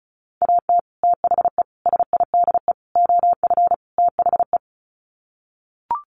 Morse code, heard as a keyed tone, about 34 wpm, 700 Hz. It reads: AT THE SIDE OF THE
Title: Sets of 5 Words Encore2 35wpm